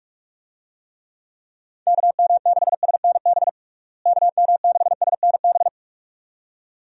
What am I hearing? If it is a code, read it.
Morse code, heard as a keyed tone, about 46 wpm, 700 Hz. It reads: KM6SNB KM6SNB
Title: USA Call Signs - 6 Characters - Repeated 2x - 45wpm